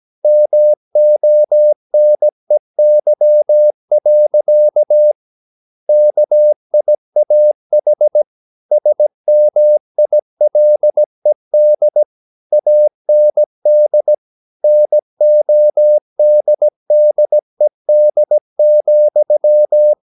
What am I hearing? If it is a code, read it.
Morse code, heard as a keyed tone, about 17 wpm, 600 Hz. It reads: MONEY. KIAH SMILED AND NODDED,
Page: Romanen Never af Ken Follet på engelsk.